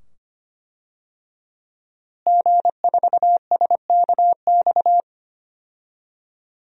Morse code, heard as a keyed tone, about 25 wpm, 700 Hz. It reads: G4SKX
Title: DXCC-100 Call Signs - 5 Characters - Repeated 2x - 25wpm